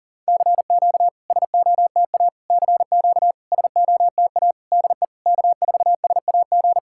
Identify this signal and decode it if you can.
Morse code, heard as a keyed tone, about 40 wpm, 700 Hz. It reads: CQ SOTA CQ SOTA DE K4SAG
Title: SOTA QSO Practice 40wpm